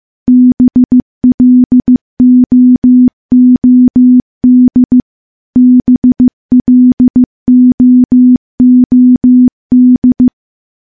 Morse code, heard as a keyed tone, about 16 wpm, 250 Hz. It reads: BLOOD BLOOD
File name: morse-training-15WPM